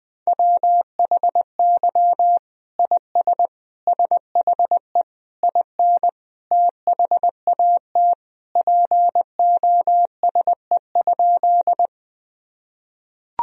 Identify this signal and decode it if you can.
Morse code, heard as a keyed tone, about 20 wpm, 700 Hz. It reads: WHY IS SHE IN THAT POSE?
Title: Sentences from Top 1000 Words - Review Part 2 - 20wpm